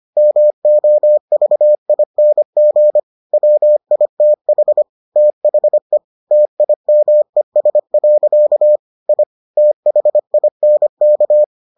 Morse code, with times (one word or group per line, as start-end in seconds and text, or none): 0.2-3.0 MOVING
3.3-4.8 WITH
5.2-6.0 THE
6.3-8.8 TIMES.
9.1-9.2 I
9.6-11.4 THINK